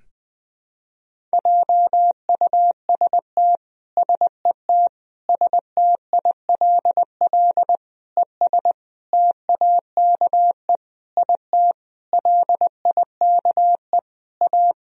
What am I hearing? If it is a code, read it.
Morse code, heard as a keyed tone, about 20 wpm, 700 Hz. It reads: JUST SET STILL ES TAKE IT LIKE A
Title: Sentences from Top 300 Words and Abbreviations 20wpm